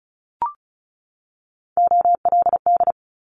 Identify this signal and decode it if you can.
Morse code, heard as a keyed tone, about 36 wpm, 700 Hz. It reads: OLD
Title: Multi-letter Phonemes 35wpm